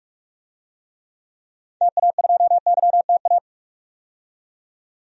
Morse code, heard as a keyed tone, about 42 wpm, 700 Hz. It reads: TA2YTA
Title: DXCC-100 Call Signs - 6 Characters - 45wpm